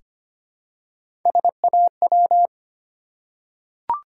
Morse code, heard as a keyed tone, about 25 wpm, 690 Hz. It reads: SAW
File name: practice / 25wpm / Easy Words with Hints - 25wpm